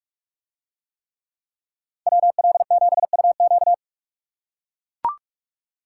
Morse code, heard as a keyed tone, about 43 wpm, 700 Hz. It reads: WL7UQ